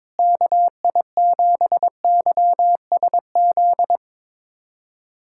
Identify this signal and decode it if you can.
Morse code, heard as a keyed tone, about 23 wpm, 700 Hz. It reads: KI7YSZ
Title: USA Callsign DE Callsign - 6 Characters - 22wpm